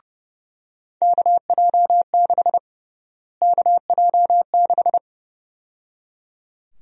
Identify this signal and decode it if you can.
Morse code, heard as a keyed tone, about 32 wpm, 700 Hz. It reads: KJ6 KJ6
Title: USA Call Sign Prefixes - Repeated 2x - 30wpm